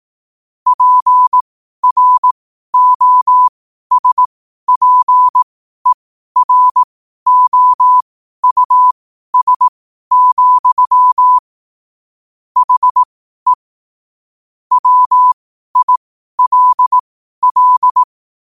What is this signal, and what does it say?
Morse code, heard as a keyed tone, about 18 wpm, spaced as for 12 wpm, 1000 Hz. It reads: PROSPEROUS, HE WILL
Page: 12 wpm morse code quotes